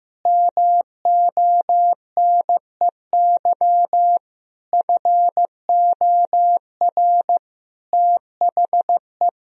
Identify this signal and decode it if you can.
Morse code, heard as a keyed tone, about 15 wpm, 700 Hz. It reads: MONEY FOR THE
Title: Sentences from Top 1000 Words - Review Part 4 - 15wpm